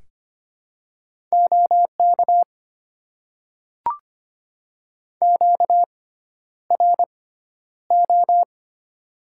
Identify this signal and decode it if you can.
Morse code, heard as a keyed tone, about 25 wpm, spaced as for 11 wpm, 700 Hz. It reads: OK QRO
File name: Top 90 QSO Elements - 6x Character Spacing - 25wpm